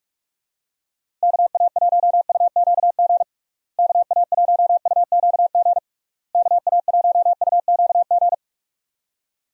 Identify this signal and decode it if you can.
Morse code, heard as a keyed tone, about 46 wpm, 700 Hz. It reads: KA1UQG KA1UQG KA1UQG
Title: USA Call Signs - 6 Characters - Repeated 3x - 45wpm